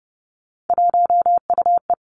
Morse code, heard as a keyed tone, about 30 wpm, 700 Hz. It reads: 1UE
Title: Three Challenging Letter-Number Combinations 30wpm